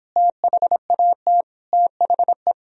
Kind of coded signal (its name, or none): Morse code